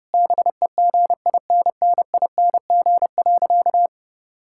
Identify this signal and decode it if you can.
Morse code, heard as a keyed tone, about 30 wpm, 700 Hz. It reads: BEGINNING.
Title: Sentences from Top 1000 Words - Review Part 5 - 30wpm